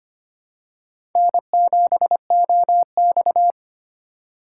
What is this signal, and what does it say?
Morse code, heard as a keyed tone, about 25 wpm, 700 Hz. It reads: N7OX
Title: USA Call Signs - 4 Characters - 4x Character Spacing - 25wpm